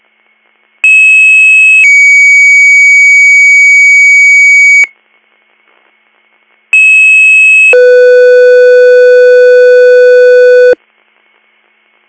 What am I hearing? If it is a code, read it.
Two-tone page, heard as a one-second tone, then a three-second tone, about 2600 then 2250 Hz, about 2600 then 500 Hz. It is heard from 0.8 to 10.7 s.